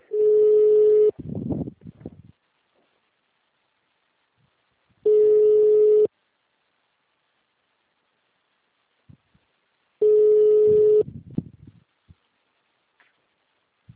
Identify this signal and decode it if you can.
Ringback tone (Europe, 425 Hz).
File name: out- -2002- - - .645